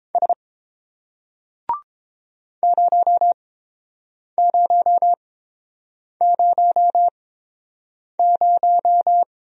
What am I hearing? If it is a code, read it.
Morse code, heard as a keyed tone, about 30 wpm, spaced as for 7 wpm, 700 Hz. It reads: S 0000